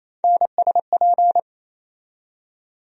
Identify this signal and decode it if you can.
Morse code, heard as a keyed tone, about 28 wpm, 700 Hz. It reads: NSP
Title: USA Call Sign Postfixes - 3 Letters 28wpm